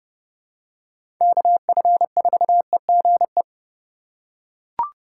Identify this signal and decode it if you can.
Morse code, heard as a keyed tone, about 30 wpm, 700 Hz. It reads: KF4EGE
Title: USA Call Signs - 6 Characters - 4x Character Spacing - 30wpm